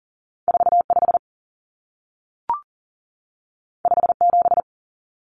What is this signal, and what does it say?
Morse code, heard as a keyed tone, about 40 wpm, 700 Hz. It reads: 45 57
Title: Numbers - 2-Digit 40wpm